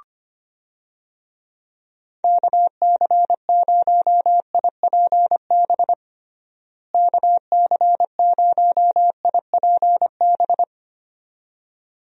Morse code, heard as a keyed tone, about 25 wpm, 700 Hz. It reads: KC0IPB KC0IPB
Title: USA Call Signs - 6 Characters - Repeated 2x - 25wpm